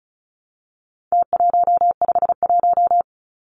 Morse code, heard as a keyed tone, about 34 wpm, 700 Hz. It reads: T151
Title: Numbers - Contest Style - 4-Digit 35wpm